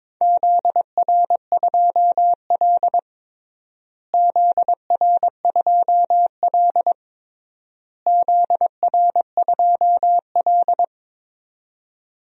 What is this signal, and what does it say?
Morse code, heard as a keyed tone, about 22 wpm, 700 Hz. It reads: ZR2L ZR2L ZR2L
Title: DXCC-100 Call Signs - 4 Characters - Repeated 3x - 22wpm